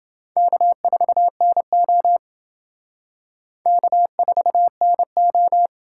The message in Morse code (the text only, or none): K4NO K4NO